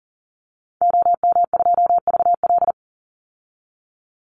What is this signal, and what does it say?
Morse code, heard as a keyed tone, about 40 wpm, 700 Hz. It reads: OM2VL
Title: Call Signs Repeated 3x - 40wpm